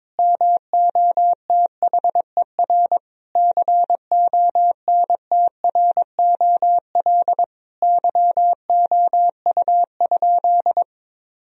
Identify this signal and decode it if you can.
Morse code, heard as a keyed tone, about 21 wpm, 700 Hz. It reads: MOTHER CONTROL YOU?
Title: Sentences from Top 1000 Words - Review Part 3 - 22wpm